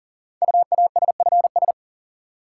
Morse code, heard as a keyed tone, about 38 wpm, 700 Hz. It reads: UASFS
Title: Code Groups - 5 Letters - 40wpm